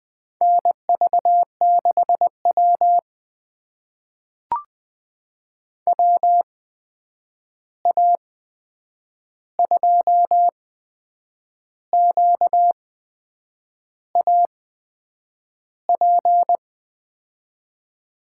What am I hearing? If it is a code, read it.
Morse code, heard as a keyed tone, about 20 wpm, spaced as for 8 wpm, 700 Hz. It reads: NV6W WA2QAP
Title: USA Call Signs - 8x Character Spacing - 20wpm